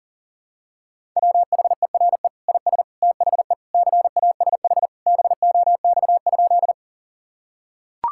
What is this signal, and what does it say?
Morse code, heard as a keyed tone, about 40 wpm, 700 Hz. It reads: WHERE IS THE CASH BOX?
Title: Sentences from Top 1000 Words - Review Part 5 - 40wpm